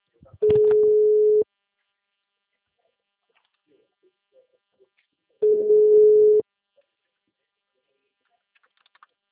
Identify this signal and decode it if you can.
Ringback tone (Europe, 425 Hz), with speech.
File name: out- -2012